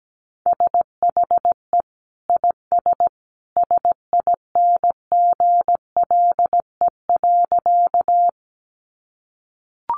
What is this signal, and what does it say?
Morse code, heard as a keyed tone, about 17 wpm, 700 Hz. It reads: SHE IS SINGLE.
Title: Sentences from Top 700 Words 17wpm